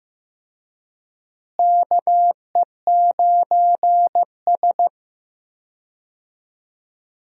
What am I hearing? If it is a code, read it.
Morse code, heard as a keyed tone, about 15 wpm, 700 Hz. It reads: KE9S
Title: USA Call Signs - 4 Characters - Repeated 2x - 15wpm